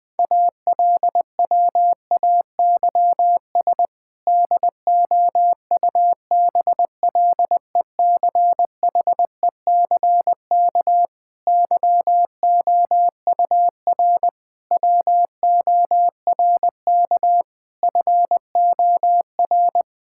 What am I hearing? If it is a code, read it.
Morse code, heard as a keyed tone, about 20 wpm, 700 Hz. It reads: ALWAYS DOUBLECHECK YOUR WORK FOR
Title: Difficult Sentences with Hints - 20wpm